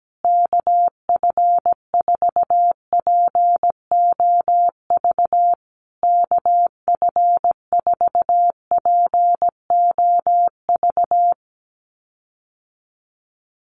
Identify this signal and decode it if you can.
Morse code, heard as a keyed tone, about 17 wpm, 700 Hz. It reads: KF4POV KF4POV
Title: USA Call Signs Repeated 3x 17wpm